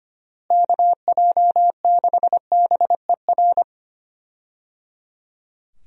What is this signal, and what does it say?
Morse code, heard as a keyed tone, about 25 wpm, 700 Hz. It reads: KJ6BER
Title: USA Call Signs - 6 Characters - Repeated 3x - 25wpm